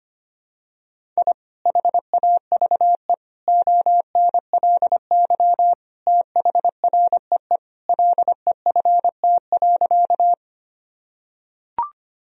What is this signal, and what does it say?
Morse code, heard as a keyed tone, about 25 wpm, 700 Hz. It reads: I HAVE ONLY THREE LEFT.
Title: Sentences from Top 1000 Words - Review Part 1 - 25wpm